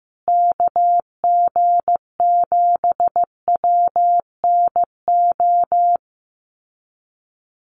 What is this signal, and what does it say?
Morse code, heard as a keyed tone, about 15 wpm, 700 Hz. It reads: KG7WNO
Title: USA Call Signs Repeated 2x 15wpm